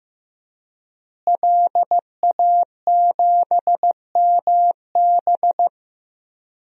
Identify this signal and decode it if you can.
Morse code, heard as a keyed tone, about 15 wpm, 700 Hz. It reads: LA7MB
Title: DXCC-100 Call Signs - 5 Characters - Repeated 2x - 15wpm